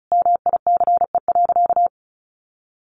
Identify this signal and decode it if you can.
Morse code, heard as a keyed tone, about 34 wpm, 700 Hz. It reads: MICE.
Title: A Sound Of Thunder 35wpm